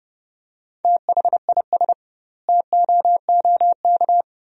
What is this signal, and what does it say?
Morse code, heard as a keyed tone, about 32 wpm, 700 Hz. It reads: THIS TOOK
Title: 2x Word-Spacing - Sentences from Top 300 Words - 30wpm